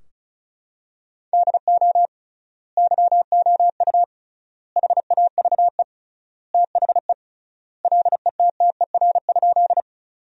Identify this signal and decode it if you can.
Morse code, heard as a keyed tone, about 36 wpm, 700 Hz. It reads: DO YOU HAVE THE LETTER?
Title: 3x Word-Spacing - Sentences from Top 300 Words - 35wpm